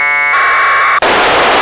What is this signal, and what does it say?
Packet radio (AX.25, as in APRS).